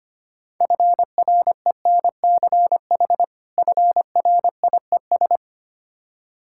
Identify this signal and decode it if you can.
Morse code, heard as a keyed tone, about 25 wpm, 690 Hz. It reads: FRENCH FRIES
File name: practice / 25wpm / Difficult Words - Speed-Racing with Hints - 25wpm